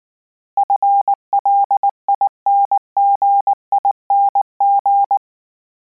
Morse code, heard as a keyed tone, about 19 wpm, 800 Hz. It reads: FLINGING
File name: Combinations / Combinations-mp3s / Combinations@18